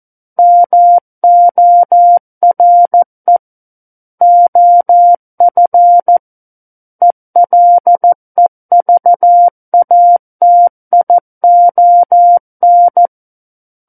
Morse code, heard as a keyed tone, about 14 wpm, 700 Hz. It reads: MORE OF ELEVATION
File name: Book One / 05-Chapter 5 - 14 WPM